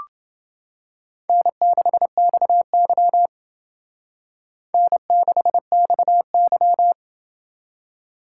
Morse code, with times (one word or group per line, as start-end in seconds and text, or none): 1.3-3.3 N6XY
4.7-6.9 N6XY